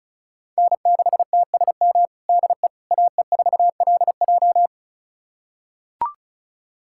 Morse code, heard as a keyed tone, about 34 wpm, 700 Hz. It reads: N6TSM DE AE4LJ
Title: USA Callsign DE Callsign - 5 Characters - 35wpm